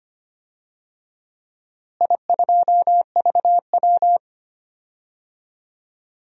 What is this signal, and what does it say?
Morse code, heard as a keyed tone, about 25 wpm, 700 Hz. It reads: I2VW
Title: DXCC-100 Call Signs - 4 Characters - Repeated 2x - 25wpm